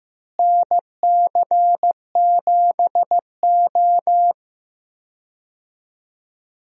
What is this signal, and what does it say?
Morse code, heard as a keyed tone, about 15 wpm, 700 Hz. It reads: NC7O
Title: USA Call Signs - 4 Characters - Repeated 3x - 15wpm